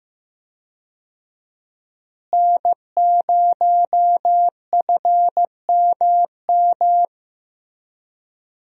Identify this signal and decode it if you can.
Morse code, heard as a keyed tone, about 15 wpm, 700 Hz. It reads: N0FMM
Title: USA Call Signs - 5 Characters - Repeated 3x - 15wpm